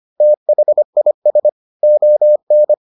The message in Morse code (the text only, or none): THIS ON